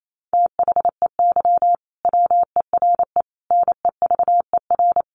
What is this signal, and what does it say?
Morse code, heard as a keyed tone, about 28 wpm, 700 Hz. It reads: THEY WERE NEVER
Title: Sentences from Top 1400 Words 28wpm